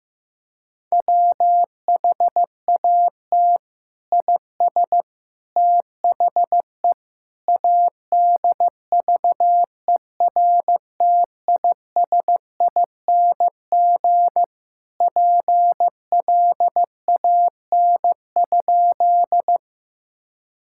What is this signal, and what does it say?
Morse code, heard as a keyed tone, about 15 wpm, 700 Hz. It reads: WHAT IS THE ADVERTISING PLAN?
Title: Sentences from Top 1000 Words - Review Part 6 - 15wpm